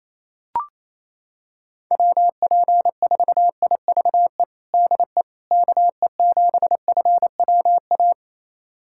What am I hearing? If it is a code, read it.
Morse code, heard as a keyed tone, about 27 wpm, 700 Hz. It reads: WP4IVE DE KE7FWA
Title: USA Callsign DE Callsign 28wpm